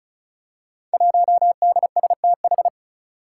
Morse code, heard as a keyed tone, about 36 wpm, 700 Hz. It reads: 1DSTH
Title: Five Challenging Letter-Number Combinations 35wpm